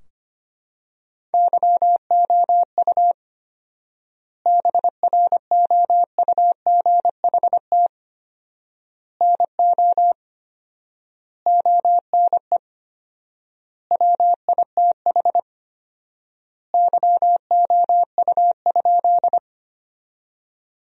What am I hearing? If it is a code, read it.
Morse code, heard as a keyed tone, about 25 wpm, 700 Hz. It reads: YOU BROUGHT NO ONE WITH YOU?
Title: 4x Word-Spacing - Sentences from Top 500 Words - 25wpm